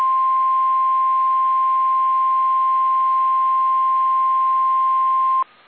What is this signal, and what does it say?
Emergency Alert System (SAME) tones (the 1050 Hz attention tone). It is heard from 0.3 to 5.2 s.
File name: eas / KSDZKDJL / KSDZKDJL